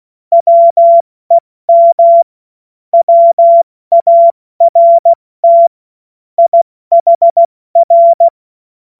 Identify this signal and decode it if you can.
Morse code, heard as a keyed tone, about 16 wpm, 680 Hz. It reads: WEM WART IHR
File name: Morsekurs / Witze / Witz 19. Oktober 2021, 70 F80